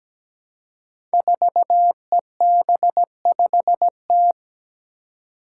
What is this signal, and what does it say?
Morse code, heard as a keyed tone, about 17 wpm, 700 Hz. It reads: 4EB5T
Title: Five Challenging Letter-Number Combinations 17wpm